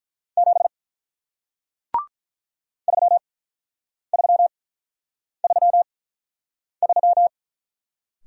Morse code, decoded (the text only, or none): 7 3333